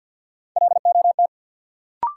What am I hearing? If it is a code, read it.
Morse code, heard as a keyed tone, about 48 wpm, 700 Hz. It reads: LOT
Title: Sentences from Top 500 Words - New Word Drill - 50wpm